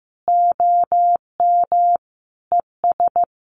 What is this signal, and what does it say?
Morse code, heard as a keyed tone, about 15 wpm, 700 Hz. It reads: OM ES
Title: Common QSO Phrases 15wpm